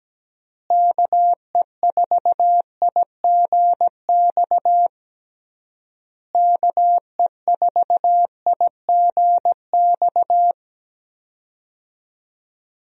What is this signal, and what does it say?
Morse code, heard as a keyed tone, about 17 wpm, 700 Hz. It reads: KE4IGX KE4IGX
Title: USA Call Signs - 6 Characters - Repeated 3x - 17wpm